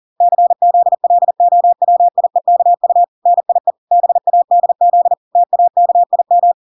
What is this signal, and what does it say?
Morse code, heard as a keyed tone, about 40 wpm, 700 Hz. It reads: CZLOWIEKU NIE BADZ TAKIM